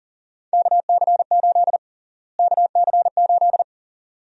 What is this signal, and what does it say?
Morse code, heard as a keyed tone, about 40 wpm, 700 Hz. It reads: KC8 KC8
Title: USA Call Sign Prefixes - Repeated 2x - 40wpm